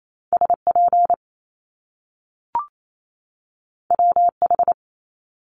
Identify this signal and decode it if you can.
Morse code, heard as a keyed tone, about 28 wpm, 700 Hz. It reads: SP WH
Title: Top Two Letter Combinations in Words 28wpm